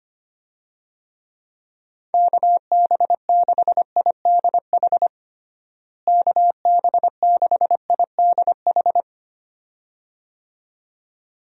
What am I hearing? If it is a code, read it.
Morse code, heard as a keyed tone, about 25 wpm, 700 Hz. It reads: KB6IDH KB6IDH
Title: USA Call Signs - 6 Characters - Repeated 2x - 25wpm